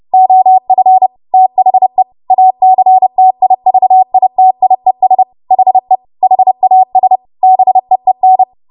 Morse code, heard as a keyed tone, about 30 wpm, 750 Hz. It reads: OF THE ACTIVITIES HE HAS BEEN
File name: Morse / 30WPM